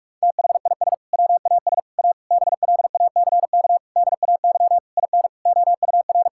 Morse code, heard as a keyed tone, about 41 wpm, 700 Hz. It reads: THIS WAS A BLACK DAY IN OUR
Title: Sentences from Top 1600 Words 45wpm